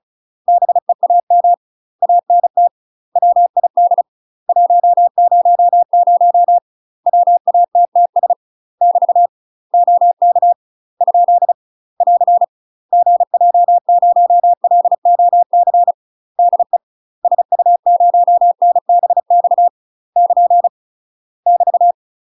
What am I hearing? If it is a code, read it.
Morse code, heard as a keyed tone, about 36 wpm, 700 Hz. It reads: BEAM ANT WID 100 WATTS <BT> OK ? <AR> GJ0LOC DE SU0NBX <KN> <BT>
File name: UK MASTER QSOs TRAINING / 35WPM / UK MASTER QSOs 0035